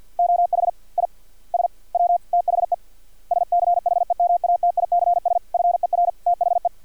Morse code, heard as a keyed tone, about 50 wpm, 700 Hz. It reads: OH I S W THE SCHEMATICS FER THE
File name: archived / CRSnet